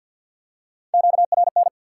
Morse code, heard as a keyed tone, about 48 wpm, 700 Hz. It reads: QRN
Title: Top 180 QSO Elements 50wpm